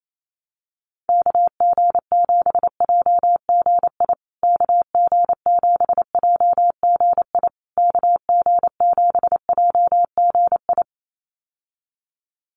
Morse code, heard as a keyed tone, about 28 wpm, 700 Hz. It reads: KG7JGI KG7JGI KG7JGI
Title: USA Call Signs Repeated 3x 28wpm